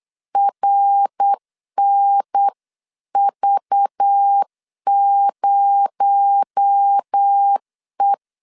Morse code, heard as a keyed tone, about 8 wpm, 800 Hz. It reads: RNV0E
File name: 8wpm